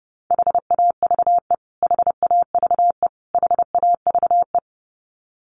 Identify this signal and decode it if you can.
Morse code, heard as a keyed tone, about 30 wpm, 700 Hz. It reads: HAVE HAVE HAVE
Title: Top 200 Words Repeated 30wpm